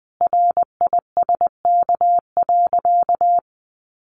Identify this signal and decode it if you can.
Morse code, heard as a keyed tone, about 19 wpm, 700 Hz. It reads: RISK.
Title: Sentences from Top 600 Words 20wpm